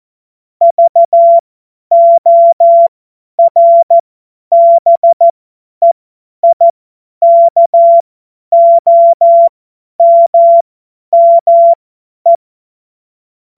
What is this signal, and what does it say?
Morse code, heard as a keyed tone, about 14 wpm, spaced as for 10 wpm, 680 Hz. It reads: VORBEIKOMME
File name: Morsekurs / Witze / Witz 02 Januar 2024 50 F70